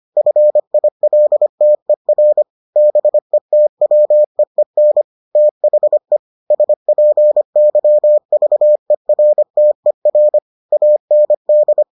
Morse code, heard as a keyed tone, about 25 wpm, 600 Hz. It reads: FILTER BETWEEN THE SPYVERTER AND